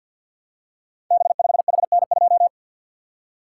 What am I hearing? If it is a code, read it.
Morse code, heard as a keyed tone, about 48 wpm, 700 Hz. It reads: B5HN2
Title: Code Groups - 5 Letter:Numbers - 50wpm